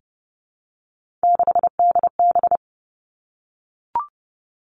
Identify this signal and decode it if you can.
Morse code, heard as a keyed tone, about 30 wpm, 700 Hz. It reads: 6DB
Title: Three Challenging Letter-Number Combinations 30wpm